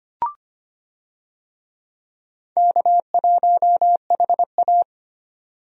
Morse code, heard as a keyed tone, about 25 wpm, 700 Hz. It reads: K1HA
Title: USA Call Signs - 4 Characters - 25wpm